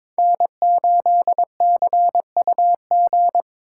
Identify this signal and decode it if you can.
Morse code, heard as a keyed tone, about 22 wpm, 700 Hz. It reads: N8CUG
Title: USA Call Signs - 5 Characters - 2x Character Spacing - 22wpm